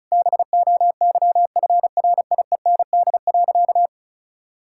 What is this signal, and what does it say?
Morse code, heard as a keyed tone, about 36 wpm, 700 Hz. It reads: BOYFRIEND.
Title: Sentences from Top 1000 Words - Review Part 5 - 35wpm